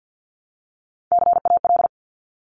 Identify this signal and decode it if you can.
Morse code, heard as a keyed tone, about 50 wpm, 700 Hz. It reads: CAL
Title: Top Three Letter Combinations in Words 50wpm